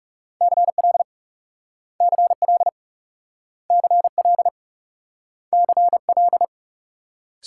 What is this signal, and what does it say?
Morse code, heard as a keyed tone, about 41 wpm, 700 Hz. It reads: CL CL CL CL